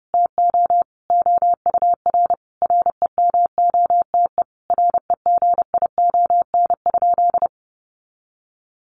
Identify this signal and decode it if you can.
Morse code, heard as a keyed tone, about 28 wpm, 700 Hz. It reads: TO OUR REMOTE REGION?
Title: Sentences from Top 1500 Words 30wpm